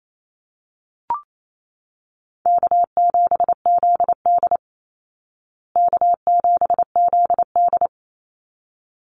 Morse code, heard as a keyed tone, about 28 wpm, 700 Hz. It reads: K7ZD K7ZD
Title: Easy Call Signs Repeated 2x - 28wpm